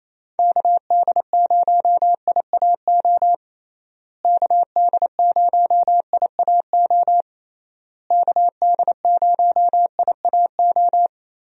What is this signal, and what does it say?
Morse code, heard as a keyed tone, about 28 wpm, 700 Hz. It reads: KD0IAO KD0IAO KD0IAO
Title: USA Call Signs - 6 Characters - Repeated 3x - 28wpm